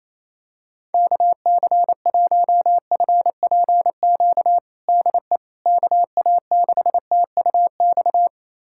Morse code, heard as a keyed tone, about 27 wpm, 700 Hz. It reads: KC1FPQ DE KA6TUX
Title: USA Callsign DE Callsign 28wpm